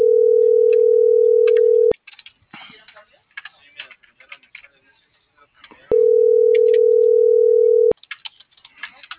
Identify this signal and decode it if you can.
Ringback tone (North America), 440 + 480 Hz.